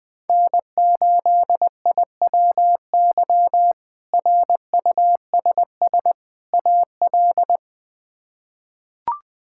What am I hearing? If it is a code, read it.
Morse code, heard as a keyed tone, about 20 wpm, 700 Hz. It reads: N8IWY RUSS AL
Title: K1USN SST QSO Practice - 20wpm